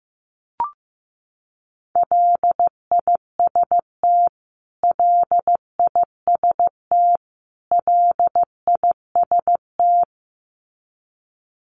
Morse code, heard as a keyed tone, about 15 wpm, 700 Hz. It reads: LIST LIST LIST
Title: Top 500 Words Repeated 15wpm